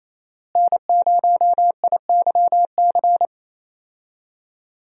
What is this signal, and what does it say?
Morse code, heard as a keyed tone, about 28 wpm, 700 Hz. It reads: N0IYC
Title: USA Call Signs - 5 Characters - Repeated 3x - 28wpm